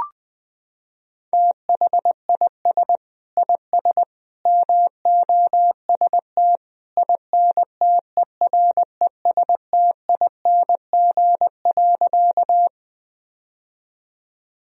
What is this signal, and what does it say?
Morse code, heard as a keyed tone, about 20 wpm, 700 Hz. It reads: THIS IS MOST INTERESTING.
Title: Sentences from Top 1000 Words - Review Part 1 - 20wpm